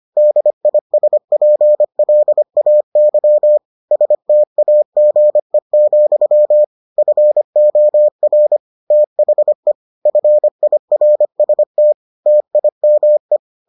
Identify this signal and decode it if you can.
Morse code, heard as a keyed tone, about 25 wpm, 600 Hz. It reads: DISPLAY STAGE, FOR THE FIRST TIME